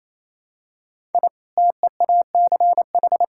Morse code, heard as a keyed tone, about 27 wpm, 700 Hz. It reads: I TEACH
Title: Sentences from Top 1000 Words - Review Part 6 - 28wpm